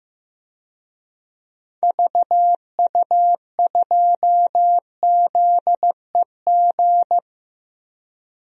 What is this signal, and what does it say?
Morse code, heard as a keyed tone, about 15 wpm, 700 Hz. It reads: VU2ZEG
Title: DXCC-100 Call Signs - 6 Characters - Repeated 3x - 15wpm